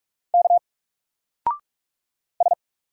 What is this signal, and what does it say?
Morse code, heard as a keyed tone, about 44 wpm, 700 Hz. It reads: K S
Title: Single Letter-Number 45wpm